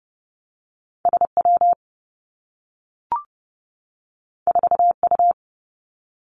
Morse code, heard as a keyed tone, about 31 wpm, 700 Hz. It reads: S W 4 U
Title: Two Challenging Letter-Number Combinations 30wpm